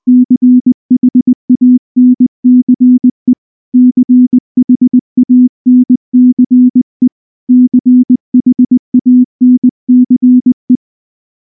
Morse code, heard as a keyed tone, about 19 wpm, 250 Hz. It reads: CHANCE CHANCE CHANCE
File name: morse / morse-training-20WPM-2